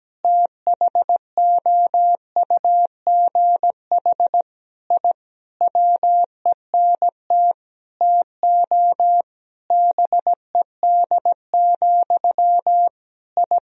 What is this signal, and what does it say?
Morse code, heard as a keyed tone, about 17 wpm, 700 Hz. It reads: THOUGH I WENT TO BED, I
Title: Sentences from Top 1000 Words - Review Part 2 - 17wpm